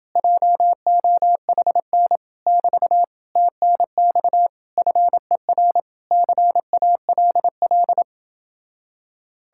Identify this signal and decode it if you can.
Morse code, heard as a keyed tone, about 31 wpm, 700 Hz. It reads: JOHN <BT> TNX FER CALL